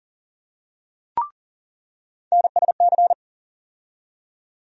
Morse code, heard as a keyed tone, about 40 wpm, 700 Hz. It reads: NSC
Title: USA Call Sign Postfixes - 3 Letters 40wpm